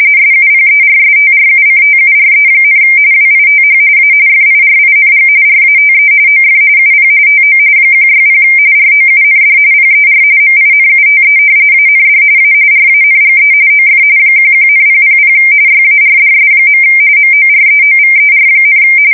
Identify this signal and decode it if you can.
RTTY, about 45.45 baud. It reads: SUPERSTRUCTURE, WITH A FIRING RATE OF UP TO 160 ROUND PER MINUTE AND A RANGE OF 11,000 YARDS. MOREOVER,
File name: fsk170 / battleshipnj